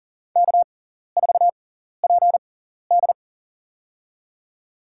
Morse code, heard as a keyed tone, about 37 wpm, spaced as for 14 wpm, 700 Hz. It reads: K4PD
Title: USA Call Signs - 4 Characters - 6x Character Spacing - 40wpm